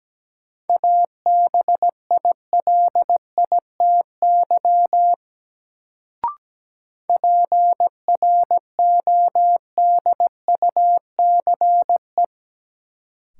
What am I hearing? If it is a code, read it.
Morse code, heard as a keyed tone, about 17 wpm, 700 Hz. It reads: ABILITY PRODUCE
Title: Sentences from Top 1000 Words - Review Part 3 - New Word Drill - 17wpm